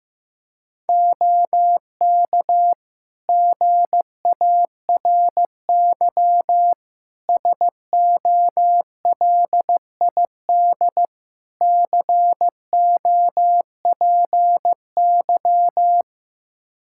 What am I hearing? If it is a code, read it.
Morse code, heard as a keyed tone, about 16 wpm, 700 Hz. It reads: OK GARY SOLID COPY
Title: Practice QSO - Exchange 3 - More Info - 15wpm